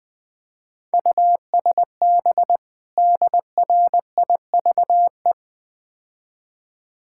Morse code, heard as a keyed tone, about 22 wpm, 700 Hz. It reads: USB DRIVE
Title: Easy Words - Speed-Racing with Hints - 20wpm